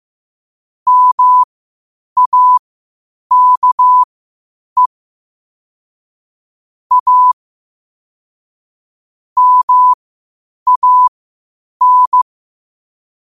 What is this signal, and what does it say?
Morse code, heard as a keyed tone, about 15 wpm, spaced as for 8 wpm, 1000 Hz. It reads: MAKE A MAN
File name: qotdfiles / 8wpm